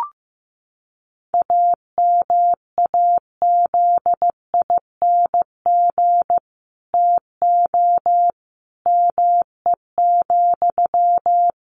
Morse code, heard as a keyed tone, about 15 wpm, 700 Hz. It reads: AMAZING TO ME,
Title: Sentences from Top 1500 Words 15wpm